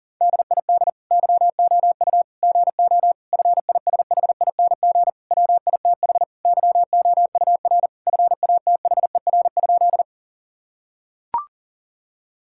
Morse code, heard as a keyed tone, about 40 wpm, 700 Hz. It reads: DID YOU GO FISHING WITH YOUR FATHER?
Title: Sentences from Top 1000 Words - Review Part 5 - 40wpm